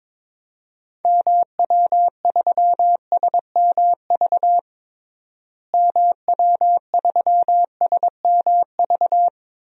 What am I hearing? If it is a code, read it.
Morse code, heard as a keyed tone, about 21 wpm, 700 Hz. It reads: MW3SMV MW3SMV
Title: DXCC-100 Call Signs - 6 Characters - Repeated 2x - 22wpm